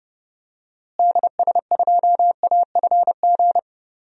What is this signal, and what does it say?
Morse code, heard as a keyed tone, about 32 wpm, 700 Hz. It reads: DS2AFG
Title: DXCC-100 Call Signs - 6 Characters - Repeated 2x - 30wpm